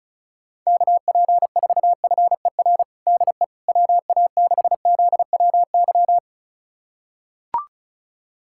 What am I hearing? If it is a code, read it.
Morse code, heard as a keyed tone, about 36 wpm, 700 Hz. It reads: KP4FER DE WA6ZWY
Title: USA Callsign DE Callsign - 6 Characters - 35wpm